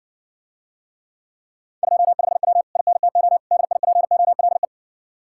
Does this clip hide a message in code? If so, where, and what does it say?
Yes: Morse code from 1.8 to 4.6 s, heard as a keyed tone, about 61 wpm, 700 Hz. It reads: 25W INTO DIPOLE